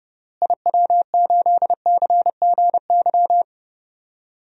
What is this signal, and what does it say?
Morse code, heard as a keyed tone, about 30 wpm, 700 Hz. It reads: IW8CGY
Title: DXCC-100 Call Signs - 6 Characters - Repeated 3x - 30wpm